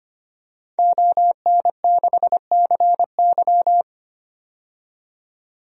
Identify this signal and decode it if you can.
Morse code, heard as a keyed tone, about 25 wpm, 700 Hz. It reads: ON6CY
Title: DXCC-100 Call Signs - 5 Characters - 25wpm